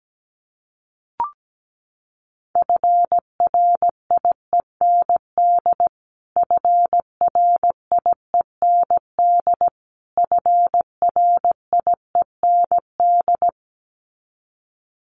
Morse code, heard as a keyed tone, about 17 wpm, 700 Hz. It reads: FRIEND FRIEND FRIEND
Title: Top 300 Words Repeated 17wpm